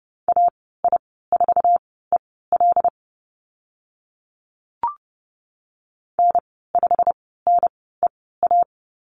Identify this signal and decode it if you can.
Morse code, heard as a keyed tone, about 30 wpm, spaced as for 17 wpm, 690 Hz. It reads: AI4EL N5NEA
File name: Lesson 15g - Number 5 - Call Signs 30wpm